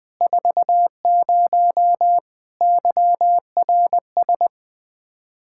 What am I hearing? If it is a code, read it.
Morse code, heard as a keyed tone, about 20 wpm, 700 Hz. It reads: 40 YRS
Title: Practice QSO - Exchange 3 - More Info - 20wpm